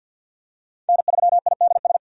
Morse code, heard as a keyed tone, about 52 wpm, 700 Hz. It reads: N3IDS
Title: USA Call Signs - 5 Characters - 50wpm